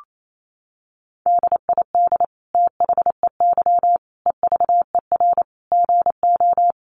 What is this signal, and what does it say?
Morse code, heard as a keyed tone, about 28 wpm, 700 Hz. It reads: DID THEY EVER GO
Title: Sentences from Top 400 Words 28wpm